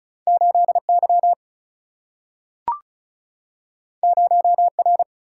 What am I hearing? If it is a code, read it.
Morse code, heard as a keyed tone, about 34 wpm, 700 Hz. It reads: 8Y 0R
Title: Single-Letter Single-Number 35wpm